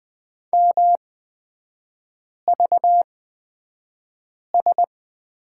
Morse code, heard as a keyed tone, about 20 wpm, spaced as for 5 wpm, 700 Hz. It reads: MVS
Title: 250 Lesson 40e - BK - Three Characters 20wpm